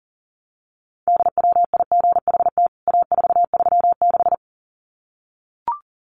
Morse code, heard as a keyed tone, about 37 wpm, 700 Hz. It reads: DWIGHT A436
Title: CWT Exchange 40wpm